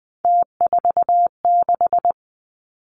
Morse code, heard as a keyed tone, about 20 wpm, 700 Hz. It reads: T46
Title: Three Challenging Letter-Number Combinations 20wpm